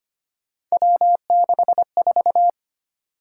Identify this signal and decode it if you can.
Morse code, heard as a keyed tone, about 29 wpm, 700 Hz. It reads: W64
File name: Speed-Racing - Three Letter-Number Combinations - 25wpm